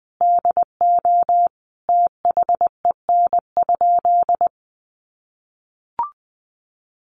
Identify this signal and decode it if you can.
Morse code, heard as a keyed tone, about 20 wpm, 700 Hz. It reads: DO THEN?
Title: Sentences from Top 100 Words 20wpm